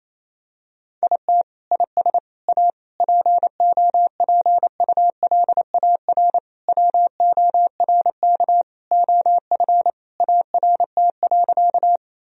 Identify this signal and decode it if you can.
Morse code, heard as a keyed tone, about 28 wpm, 700 Hz. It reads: IT IS A POPULAR WORK OF ART.
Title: Sentences from Top 1000 Words - Review Part 3 - 28wpm